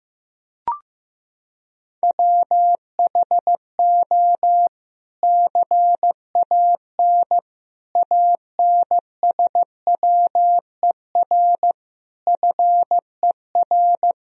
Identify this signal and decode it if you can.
Morse code, heard as a keyed tone, about 16 wpm, 700 Hz. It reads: WHO CAN ANSWER FER
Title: Sentences from Top 300 Words and Abbreviations 15wpm